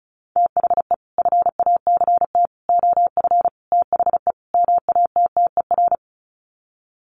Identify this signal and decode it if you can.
Morse code, heard as a keyed tone, about 34 wpm, 700 Hz. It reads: THE FACT OF THE MATTER
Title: Sets of 5 Words 35wpm